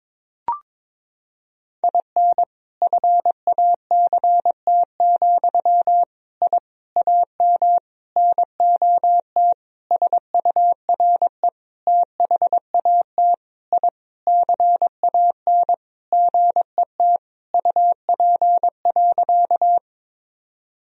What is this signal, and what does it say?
Morse code, heard as a keyed tone, about 22 wpm, 700 Hz. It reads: IN FACT, I AM NOT SURE THAT I CAN GET UP.
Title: Sentences from Top 1000 Words - Review Part 3 - 22wpm